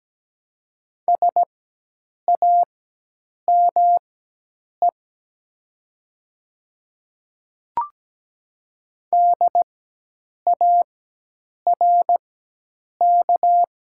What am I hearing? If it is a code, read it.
Morse code, heard as a keyed tone, about 17 wpm, spaced as for 8 wpm, 700 Hz. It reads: SAME DARK
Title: 4x Character Spacing - Top 500 Words - 17wpm